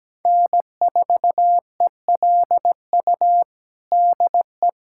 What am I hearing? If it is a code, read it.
Morse code, heard as a keyed tone, about 17 wpm, 700 Hz. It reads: N4ELU DE
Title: USA Callsign DE Callsign - 5 Characters - 17wpm